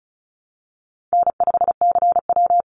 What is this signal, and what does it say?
Morse code, heard as a keyed tone, about 34 wpm, 700 Hz. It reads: N5CW
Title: Call Signs Repeated 3x - 35wpm